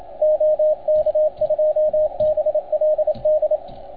Morse code, heard as a keyed tone, about 29 wpm, 690 Hz, with radio noise. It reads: OK2BLD
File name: Archiv / CUC287 / OK1FGD CUC 287